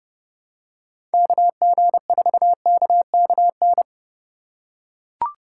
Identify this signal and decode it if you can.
Morse code, heard as a keyed tone, about 30 wpm, 700 Hz. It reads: KG4KKN
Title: USA Call Signs - 6 Characters - 2x Character Spacing - 30wpm